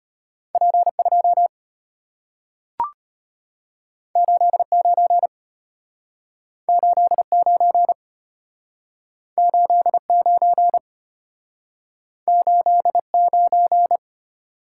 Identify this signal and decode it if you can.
Morse code, heard as a keyed tone, about 34 wpm, 690 Hz. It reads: P2 89 89 89 89